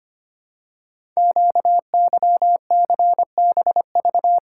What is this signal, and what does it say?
Morse code, heard as a keyed tone, about 25 wpm, 700 Hz. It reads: QYCBV
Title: Code Groups - 5 Letters - 25wpm